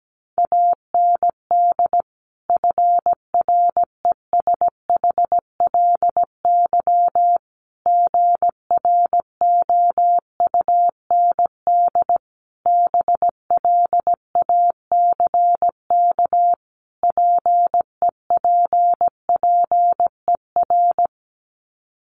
Morse code, heard as a keyed tone, about 17 wpm, 700 Hz. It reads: AND FRESHLY GROUND BLACK PEPPER
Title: Sets of 5 Words Encore 17wpm